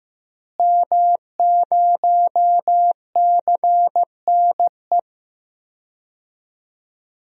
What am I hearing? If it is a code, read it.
Morse code, heard as a keyed tone, about 15 wpm, 700 Hz. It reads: M0CNE
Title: DXCC-100 Call Signs - 5 Characters - Repeated 3x - 15wpm